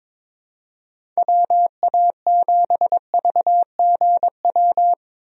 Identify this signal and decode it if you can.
Morse code, heard as a keyed tone, about 22 wpm, 700 Hz. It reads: WA7VGW
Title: USA Call Signs - 6 Characters - 22wpm